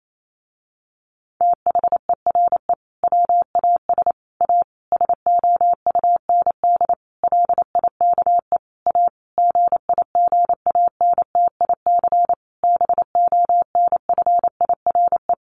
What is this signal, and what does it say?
Morse code, heard as a keyed tone, about 28 wpm, 700 Hz. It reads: THERE WAS A SOUND LIKE A GIGANTIC BONFIRE
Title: A Sound Of Thunder 28wpm